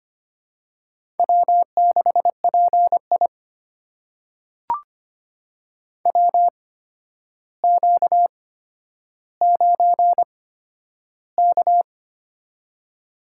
Morse code, heard as a keyed tone, about 25 wpm, spaced as for 11 wpm, 700 Hz. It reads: W6PI WQ9K
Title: USA Call Signs - 4 Characters - 8x Character Spacing - 25wpm